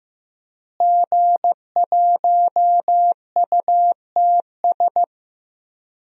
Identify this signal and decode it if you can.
Morse code, heard as a keyed tone, about 16 wpm, 700 Hz. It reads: G1UTS
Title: DXCC-100 Call Signs - 5 Characters - Repeated 2x - 15wpm